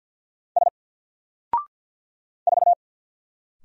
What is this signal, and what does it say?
Morse code, heard as a keyed tone, about 50 wpm, 700 Hz. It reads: S 4
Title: Single Character (Letter:Number:Punctuation:Prosign) 50wpm